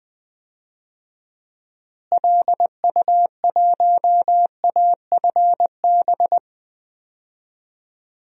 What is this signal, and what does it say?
Morse code, heard as a keyed tone, about 20 wpm, 700 Hz. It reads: LU1AFB
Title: DXCC-100 Call Signs - 6 Characters - Repeated 3x - 20wpm